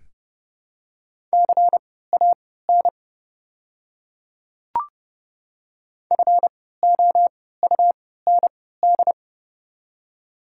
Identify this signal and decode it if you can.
Morse code, heard as a keyed tone, about 30 wpm, spaced as for 17 wpm, 700 Hz. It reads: CAN FOUND
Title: 3x Character Spacing - Top 300 Words - 30wpm